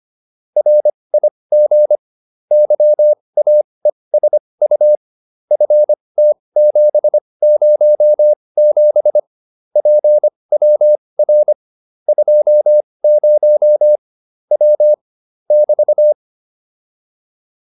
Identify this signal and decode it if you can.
Morse code, heard as a keyed tone, about 25 wpm, spaced as for 20 wpm, 600 Hz. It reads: RIG YAESU FT707 PWR 20 W <BT>